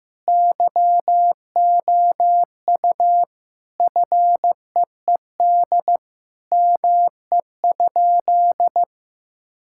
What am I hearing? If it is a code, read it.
Morse code, heard as a keyed tone, about 15 wpm, 700 Hz. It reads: YOU FEED ME?
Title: Sentences from Top 1600 Words 15wpm